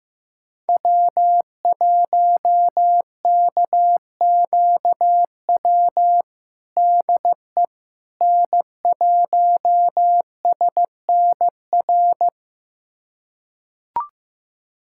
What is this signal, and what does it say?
Morse code, heard as a keyed tone, about 16 wpm, 700 Hz. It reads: W1KQW DE N1SNR
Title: USA Callsign DE Callsign - 5 Characters - 15wpm